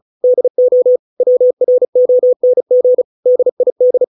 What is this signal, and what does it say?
Morse code, heard as a keyed tone, about 34 wpm, 500 Hz. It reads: DO WRONG DID